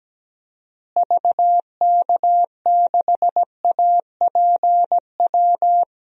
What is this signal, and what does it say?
Morse code, heard as a keyed tone, about 17 wpm, 700 Hz. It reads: VK6APW
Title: DXCC-100 Call Signs - 6 Characters - Repeated 3x - 17wpm